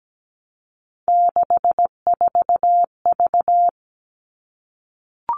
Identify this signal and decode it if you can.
Morse code, heard as a keyed tone, about 17 wpm, 700 Hz. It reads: 64V
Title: All Cut Numbers - 3-Digit 17wpm